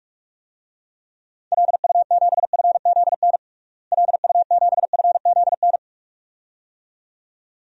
Morse code, heard as a keyed tone, about 46 wpm, 700 Hz. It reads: LU7FZN LU7FZN
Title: DXCC-100 Call Signs - 6 Characters - Repeated 2x - 45wpm